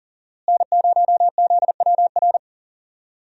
Morse code, heard as a keyed tone, about 40 wpm, 700 Hz. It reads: N0ZWR
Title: USA Call Signs - 5 Characters - Repeated 3x - 40wpm